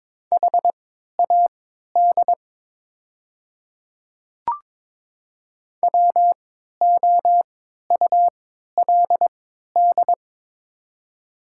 Morse code, heard as a keyed tone, about 22 wpm, spaced as for 12 wpm, 690 Hz. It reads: HAD WOULD
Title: 3x Character Spacing - Top 300 Words - 22wpm